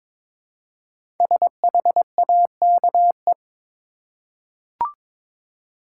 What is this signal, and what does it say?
Morse code, heard as a keyed tone, about 22 wpm, 700 Hz. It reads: SHAKE
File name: 5 Letter Words - First 3 Letters Given - 22wpm